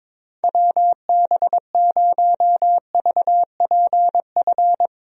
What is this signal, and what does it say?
Morse code, heard as a keyed tone, about 22 wpm, 700 Hz. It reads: WB0VPF
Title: USA Call Signs - 6 Characters - Repeated 3x - 22wpm